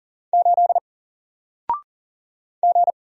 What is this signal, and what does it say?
Morse code, heard as a keyed tone, about 39 wpm, 700 Hz. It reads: 8 G
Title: Single Character (Letter:Number:Punctuation:Prosign) 40wpm